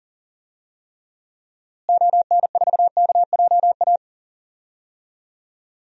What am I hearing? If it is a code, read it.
Morse code, heard as a keyed tone, about 40 wpm, 700 Hz. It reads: ON4KJA
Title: DXCC-100 Call Signs - 6 Characters - Repeated 2x - 40wpm